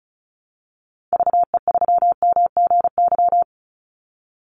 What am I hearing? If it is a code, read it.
Morse code, heard as a keyed tone, about 34 wpm, 700 Hz. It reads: VE3MGY
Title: Call Signs Repeated 3x - 35wpm